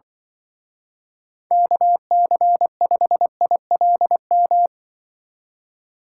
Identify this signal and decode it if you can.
Morse code, heard as a keyed tone, about 24 wpm, 700 Hz. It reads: KC5ILM
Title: Speed Racing - USA 2x3 Call Signs 20wpm